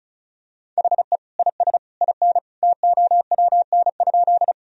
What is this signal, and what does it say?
Morse code, heard as a keyed tone, about 36 wpm, 700 Hz. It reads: HE IS IN TOWN?
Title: Sentences from Top 1000 Words - Review Part 2 - 35wpm